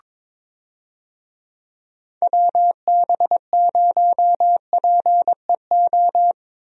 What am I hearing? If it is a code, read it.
Morse code, heard as a keyed tone, about 22 wpm, 700 Hz. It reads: WB0PEO
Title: USA Call Signs - 6 Characters - Repeated 2x - 22wpm